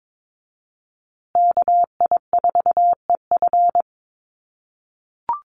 Morse code, heard as a keyed tone, about 22 wpm, 700 Hz. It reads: KI4EF
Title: USA Call Signs Repeated 2x 22wpm